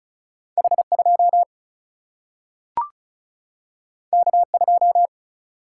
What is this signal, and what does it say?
Morse code, heard as a keyed tone, about 36 wpm, 700 Hz. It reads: H2 K2
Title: Single-Letter Single-Number 35wpm